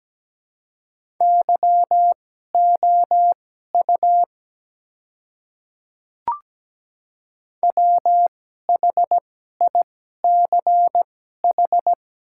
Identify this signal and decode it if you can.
Morse code, heard as a keyed tone, about 17 wpm, spaced as for 12 wpm, 700 Hz. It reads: YOU WHICH
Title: 2x Character Spacing - Top 200 Words - 17wpm